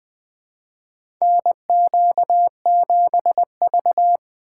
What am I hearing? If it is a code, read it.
Morse code, heard as a keyed tone, about 20 wpm, 700 Hz. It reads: NQ7V
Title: USA Call Signs - 4 Characters - 4x Character Spacing - 20wpm